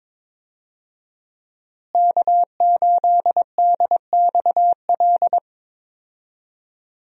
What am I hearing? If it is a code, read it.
Morse code, heard as a keyed tone, about 21 wpm, 700 Hz. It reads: K8DXL
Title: USA Call Signs - 5 Characters - Repeated 3x - 22wpm